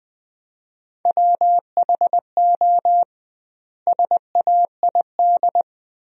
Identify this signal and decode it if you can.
Morse code, heard as a keyed tone, about 21 wpm, 700 Hz. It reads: WHO SAID
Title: 2x Word-Spacing - Sentences from Top 300 Words - 20wpm